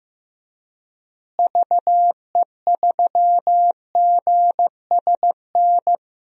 Morse code, heard as a keyed tone, about 15 wpm, 700 Hz. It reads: VE3GSN
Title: DXCC-100 Call Signs - 6 Characters - Repeated 2x - 15wpm